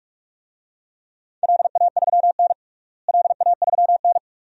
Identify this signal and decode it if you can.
Morse code, heard as a keyed tone, about 46 wpm, 700 Hz. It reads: LA3N LA3N
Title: DXCC-100 Call Signs - 4 Characters - Repeated 2x - 45wpm